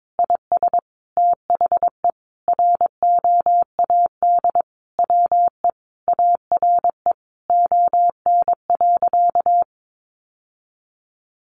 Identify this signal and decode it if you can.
Morse code, heard as a keyed tone, about 22 wpm, 700 Hz. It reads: IS THE ROAD WE ARE ON.
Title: Sentences from Top 1100 Words 22wpm